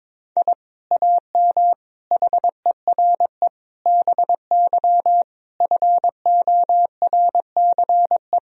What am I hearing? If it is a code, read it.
Morse code, heard as a keyed tone, about 22 wpm, 700 Hz. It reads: I AM HERE BY FORCE
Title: Sentences from Top 1000 Words - Review Part 3 - 22wpm